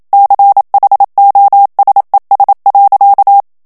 Morse code, heard as a keyed tone, about 27 wpm, 800 Hz. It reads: CHOSES.
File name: CW / FICHIERS-MP3 / FRANCAIS TEXTE CLAIR 1 VITESSE 26 MOTS MINUTES